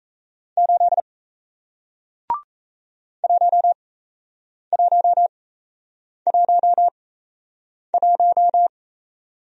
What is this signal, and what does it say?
Morse code, heard as a keyed tone, about 38 wpm, spaced as for 8 wpm, 700 Hz. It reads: 8 1111